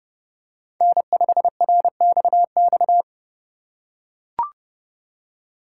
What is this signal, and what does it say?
Morse code, heard as a keyed tone, about 31 wpm, 700 Hz. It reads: N5RXX
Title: USA Call Signs - 2x Character Spacing - 30wpm